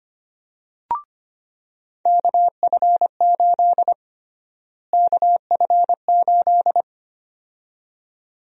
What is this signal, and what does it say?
Morse code, heard as a keyed tone, about 25 wpm, 700 Hz. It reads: KF8 KF8
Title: USA Call Sign Prefixes - Repeated 2x - 25wpm